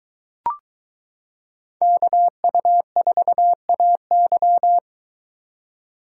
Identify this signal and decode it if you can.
Morse code, heard as a keyed tone, about 22 wpm, 700 Hz. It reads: KU4AY